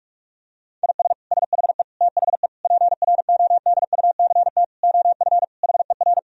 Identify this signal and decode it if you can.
Morse code, heard as a keyed tone, about 46 wpm, 700 Hz. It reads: IS SHE THE PRODUCT OF HER
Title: Sentences from Top 1000 Words - Review Part 2 - 45wpm